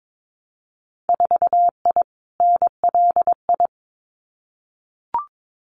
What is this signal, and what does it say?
Morse code, heard as a keyed tone, about 22 wpm, 700 Hz. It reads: 4I NLI
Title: Winter Field Day Exchanges Encore 22wpm